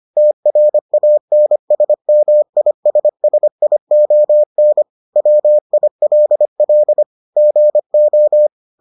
Morse code, heard as a keyed tone, about 25 wpm, 600 Hz. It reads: TRANSMISSION WILL GO